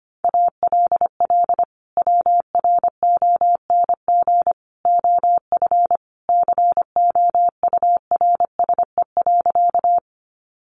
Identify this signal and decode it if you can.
Morse code, heard as a keyed tone, about 25 wpm, 700 Hz. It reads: ALL WRONG OF COURSE.
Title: Sentences from Top 900 Words 25wpm